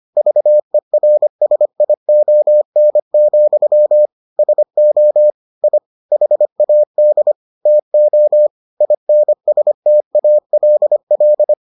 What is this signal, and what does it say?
Morse code, heard as a keyed tone, about 25 wpm, 600 Hz. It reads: VERSION, SO I HAD TO INSTALL